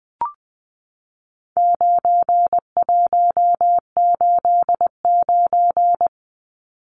Morse code, heard as a keyed tone, about 20 wpm, 700 Hz. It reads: 9189
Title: Numbers - 4-Digit 20wpm